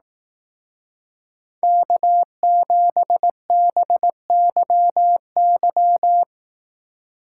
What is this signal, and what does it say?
Morse code, heard as a keyed tone, about 18 wpm, 700 Hz. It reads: K7BYY
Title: Speed Racing - USA 1x3 Call Signs 15wpm